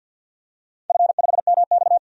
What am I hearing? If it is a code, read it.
Morse code, heard as a keyed tone, about 48 wpm, 700 Hz. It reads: F5MX
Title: DXCC-100 Call Signs - 4 Characters - 50wpm